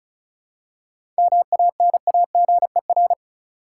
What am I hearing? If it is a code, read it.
Morse code, heard as a keyed tone, about 34 wpm, 700 Hz. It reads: MANAGER
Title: Seven Letter Words - 35wpm